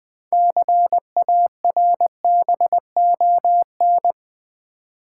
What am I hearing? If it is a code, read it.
Morse code, heard as a keyed tone, about 20 wpm, 700 Hz. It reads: CARBON
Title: 6 Letter Words - First 3 Letters Given - 20wpm